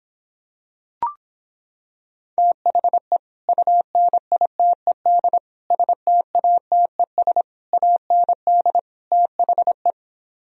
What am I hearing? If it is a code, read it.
Morse code, heard as a keyed tone, about 26 wpm, 700 Hz. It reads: THE UNITED STATES AND THE